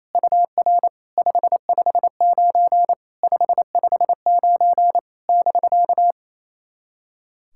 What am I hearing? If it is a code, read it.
Morse code, heard as a keyed tone, about 28 wpm, 660 Hz. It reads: UR 559 559 <BK>
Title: SOTA QSO Practice 28wpm